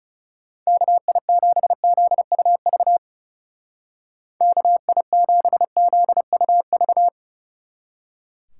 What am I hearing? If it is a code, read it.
Morse code, heard as a keyed tone, about 39 wpm, 700 Hz. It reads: KI7ZUV KI7ZUV